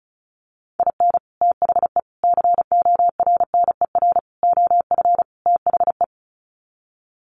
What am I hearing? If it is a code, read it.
Morse code, heard as a keyed tone, about 34 wpm, 700 Hz. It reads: IN THE CORNER OF THE
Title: Sets of 5 Words Encore 35wpm